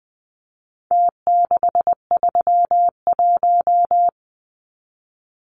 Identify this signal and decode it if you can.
Morse code, heard as a keyed tone, about 20 wpm, 700 Hz. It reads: T631
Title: Numbers - Contest Style - 4-Digit 20wpm